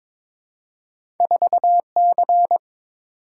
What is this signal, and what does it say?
Morse code, heard as a keyed tone, about 22 wpm, 700 Hz. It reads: 4C
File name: practice / 22wpm / Two Letter-Number Combinations - 22wpm